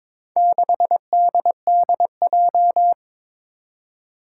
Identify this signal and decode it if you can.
Morse code, heard as a keyed tone, about 22 wpm, 700 Hz. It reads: 6DDJ
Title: Four Challenging Letter-Number Combinations 22wpm